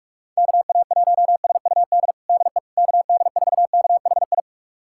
Morse code, heard as a keyed tone, about 41 wpm, 690 Hz. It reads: KA1SUD DE KD4KHI
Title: USA Callsign DE Callsign - 6 Characters - 45wpm